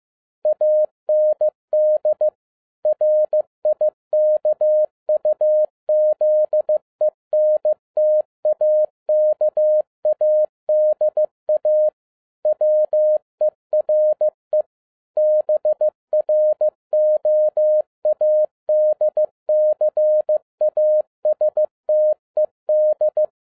Morse code, heard as a keyed tone, about 15 wpm, 610 Hz. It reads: AND RIKUZENTAKADA WERE BROADCASTED